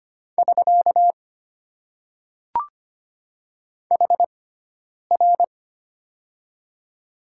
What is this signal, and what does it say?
Morse code, heard as a keyed tone, about 25 wpm, spaced as for 9 wpm, 700 Hz. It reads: <SK> HR
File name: Top 90 QSO Elements - 6x Character Spacing - 25wpm